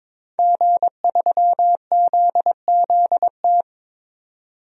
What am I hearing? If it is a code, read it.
Morse code, heard as a keyed tone, about 22 wpm, 700 Hz. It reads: G3ZZT
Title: DXCC-100 Call Signs - 5 Characters - 22wpm